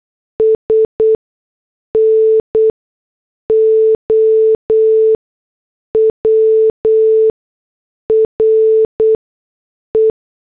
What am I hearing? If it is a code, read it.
Morse code, heard as a keyed tone, about 8 wpm, spaced as for 6 wpm, 440 Hz. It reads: S NOW RE